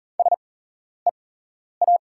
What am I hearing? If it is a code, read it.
Morse code, heard as a keyed tone, about 40 wpm, spaced as for 11 wpm, 700 Hz. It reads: SEA
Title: 8x Character Spacing - Top 300 Words - 40wpm